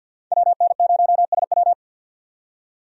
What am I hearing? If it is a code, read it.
Morse code, heard as a keyed tone, about 52 wpm, 700 Hz. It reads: WN0SW
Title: USA Call Signs - 5 Characters - 50wpm